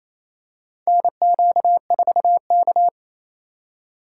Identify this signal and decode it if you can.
Morse code, heard as a keyed tone, about 28 wpm, 700 Hz. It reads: NQ4K
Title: USA Call Signs - 4 Characters - 3x Character Spacing - 28wpm